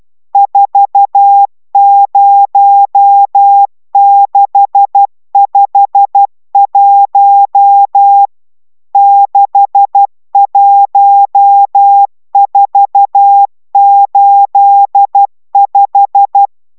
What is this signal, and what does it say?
Morse code, heard as a keyed tone, about 12 wpm, 800 Hz. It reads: 40651 61485